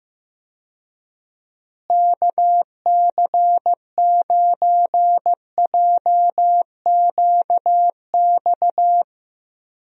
Morse code, heard as a keyed tone, about 15 wpm, 700 Hz. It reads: KC9JQX
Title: USA Call Signs - 6 Characters - Repeated 3x - 15wpm